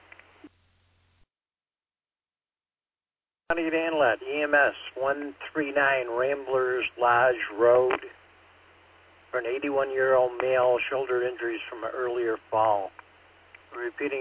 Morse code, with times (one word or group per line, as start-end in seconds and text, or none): none